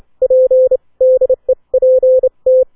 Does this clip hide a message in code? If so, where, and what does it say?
Yes: Morse code from 0.2 to 2.6 s, heard as a keyed tone, about 28 wpm, spaced as for 24 wpm, 520 Hz. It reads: P DEPT